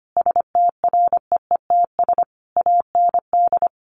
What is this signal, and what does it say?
Morse code, heard as a keyed tone, about 25 wpm, 700 Hz. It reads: STREETS AND
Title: A Sound Of Thunder 25wpm